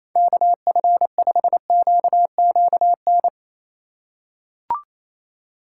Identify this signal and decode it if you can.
Morse code, heard as a keyed tone, about 28 wpm, 700 Hz. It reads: KF5QQN
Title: USA Call Signs - 6 Characters - 4x Character Spacing - 28wpm